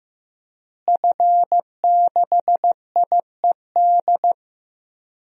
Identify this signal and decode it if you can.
Morse code, heard as a keyed tone, about 15 wpm, 700 Hz. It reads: F6IED
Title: DXCC-100 Call Signs - 5 Characters - Repeated 2x - 15wpm